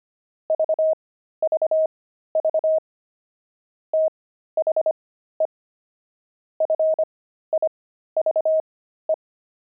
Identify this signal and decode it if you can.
Morse code, heard as a keyed tone, about 25 wpm, spaced as for 13 wpm, 630 Hz. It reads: VVV THE FIVE
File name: pangram-fs-13-cs-25-jumping